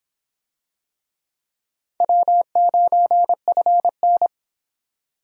Morse code, heard as a keyed tone, about 22 wpm, 700 Hz. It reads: W9FN